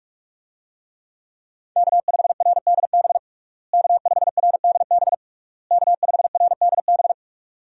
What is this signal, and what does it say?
Morse code, heard as a keyed tone, about 46 wpm, 700 Hz. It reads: K5RDB K5RDB K5RDB
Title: USA Call Signs - 5 Characters - Repeated 3x - 45wpm